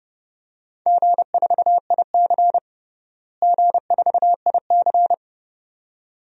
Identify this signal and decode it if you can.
Morse code, heard as a keyed tone, about 32 wpm, 700 Hz. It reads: G4IC G4IC
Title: DXCC-100 Call Signs - 4 Characters - Repeated 2x - 30wpm